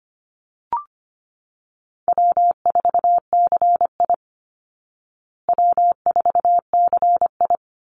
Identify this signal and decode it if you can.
Morse code, heard as a keyed tone, about 25 wpm, 700 Hz. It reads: W4CI W4CI
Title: Easy Call Signs Repeated 2x - 25wpm